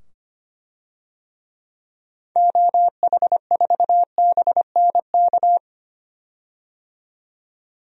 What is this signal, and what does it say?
Morse code, heard as a keyed tone, about 25 wpm, 700 Hz. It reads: OH4BNK
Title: DXCC-100 Call Signs - 6 Characters - 25wpm